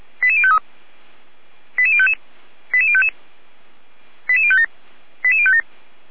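Five-tone selective call (ZVEI), heard as quick runs of five tones, tones of about 70 ms. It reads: 88053 88055 88055 88057 88057
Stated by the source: ZVEI five-tone selective call 88019 received by SDR on channel 455, from feuerwehr-orlamuende.org fire brigade audio archive.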